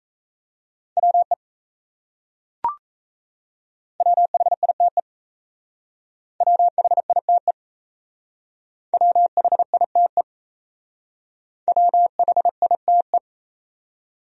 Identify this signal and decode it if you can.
Morse code, heard as a keyed tone, about 34 wpm, 700 Hz. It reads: WE WHITE WHITE WHITE WHITE
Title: Speed-Racing Top 300 Words 28wpm